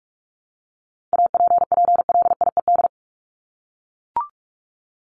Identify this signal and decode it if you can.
Morse code, heard as a keyed tone, about 44 wpm, 700 Hz. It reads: APPLIED
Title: ED Suffix - Top 100 Words with Base Words 45wpm